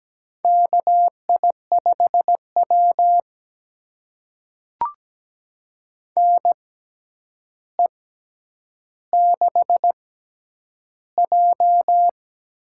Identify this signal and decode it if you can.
Morse code, heard as a keyed tone, about 17 wpm, spaced as for 9 wpm, 700 Hz. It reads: KI5W NE6J
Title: USA Call Signs - 4 Characters - 6x Character Spacing - 17wpm